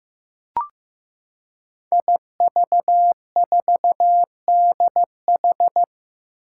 Morse code, heard as a keyed tone, about 15 wpm, 700 Hz. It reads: IV4DH
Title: Five Challenging Letter-Number Combinations 15wpm